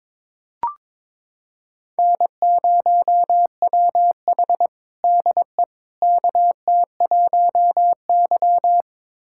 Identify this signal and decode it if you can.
Morse code, heard as a keyed tone, about 22 wpm, 700 Hz. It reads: N0WH DE KT1Y
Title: USA Callsign DE Callsign - 4 Characters - 22wpm